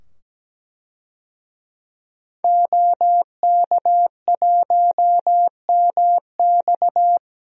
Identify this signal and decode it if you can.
Morse code, heard as a keyed tone, about 17 wpm, 700 Hz. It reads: OK1MX
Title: DXCC-100 Call Signs - 5 Characters - 17wpm